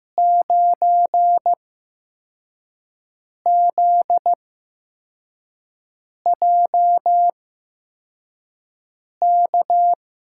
Morse code, heard as a keyed tone, about 15 wpm, spaced as for 4 wpm, 700 Hz. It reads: 9ZJK
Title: USA Call Signs - 5 Characters - 8x Character Spacing - 15wpm